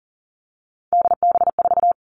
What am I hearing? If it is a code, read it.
Morse code, heard as a keyed tone, about 40 wpm, 700 Hz. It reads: DB4
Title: Three Challenging Letter-Number Combinations 40wpm